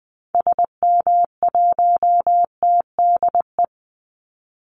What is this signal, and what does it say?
Morse code, heard as a keyed tone, about 20 wpm, 700 Hz. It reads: SM1TDE
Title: Call Signs Repeated 3x - 20wpm